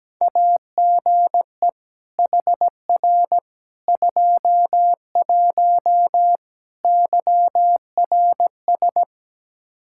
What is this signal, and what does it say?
Morse code, heard as a keyed tone, about 17 wpm, 700 Hz. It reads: AGE HR 21 YRS
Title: K1USN SST QSO Practice 17wpm